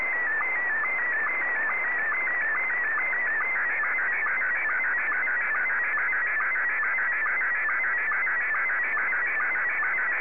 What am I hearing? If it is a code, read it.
SSTV (slow-scan television), Scottie 1.
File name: sstv / S1-VK2OY